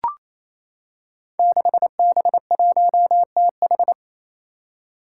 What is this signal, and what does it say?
Morse code, heard as a keyed tone, about 28 wpm, 700 Hz. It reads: 6B1TH
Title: Five Challenging Letter-Number Combinations 28wpm